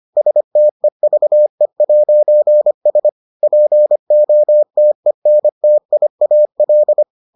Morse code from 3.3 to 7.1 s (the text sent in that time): POTENTIAL